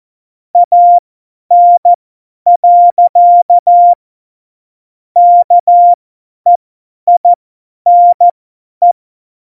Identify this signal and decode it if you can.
Morse code, heard as a keyed tone, about 14 wpm, spaced as for 10 wpm, 700 Hz. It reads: AN. KEINE
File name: Morsekurs / Witze / Witz 05. Februar 2019, 50 F70